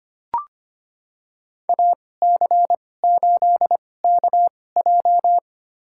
Morse code, heard as a keyed tone, about 25 wpm, spaced as for 18 wpm, 700 Hz. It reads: AC8KJ
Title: USA Call Signs - 5 Characters - 2x Character Spacing - 25wpm